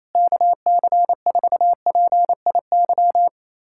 Morse code, heard as a keyed tone, about 28 wpm, 700 Hz. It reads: KC4PIY
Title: USA Callsign DE Callsign - 6 Characters - 28wpm